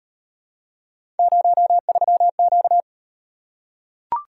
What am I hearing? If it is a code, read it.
Morse code, heard as a keyed tone, about 34 wpm, 700 Hz. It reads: 03Q